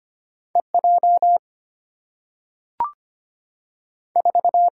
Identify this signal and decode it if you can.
Morse code, heard as a keyed tone, about 25 wpm, 700 Hz. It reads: E J 4
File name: Two Letter-Number Combinations - 25wpm